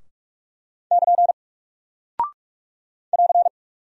Morse code, heard as a keyed tone, about 44 wpm, 700 Hz. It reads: <KN> <AR>
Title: Single Character (Letter:Number:Punctuation:Prosign) Alt 45wpm